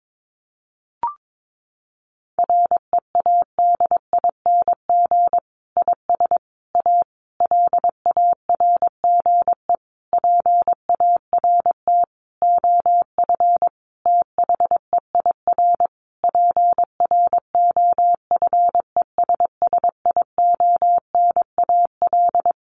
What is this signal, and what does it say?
Morse code, heard as a keyed tone, about 22 wpm, 700 Hz. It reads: READING IS A LARGE PART OF THEIR PROFESSIONAL
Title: Sentences from Top 1400 Words 22wpm